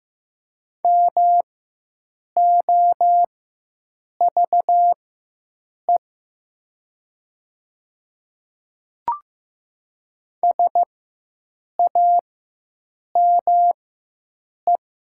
Morse code, heard as a keyed tone, about 15 wpm, spaced as for 7 wpm, 700 Hz. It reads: MOVE SAME
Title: 4x Character Spacing - Top 200 Words - 15wpm